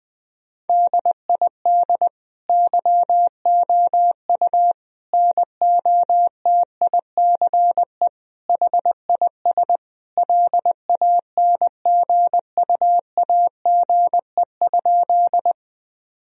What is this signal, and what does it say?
Morse code, heard as a keyed tone, about 21 wpm, 700 Hz. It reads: DID YOU NOTICE HIS LANGUAGE?
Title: Sentences from Top 1000 Words - Review Part 3 - 20wpm